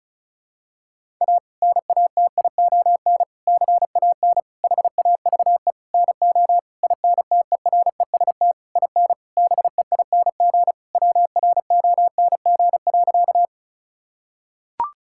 Morse code, heard as a keyed tone, about 36 wpm, 700 Hz. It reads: A NATION CAN HAVE NO INTEREST IN BEING WRONG.
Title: Sentences from Top 1000 Words - Review Part 6 - 35wpm